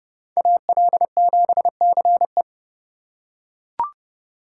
Morse code, heard as a keyed tone, about 29 wpm, 700 Hz. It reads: AL7CE
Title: USA Call Signs - 5 Characters - 2x Character Spacing - 30wpm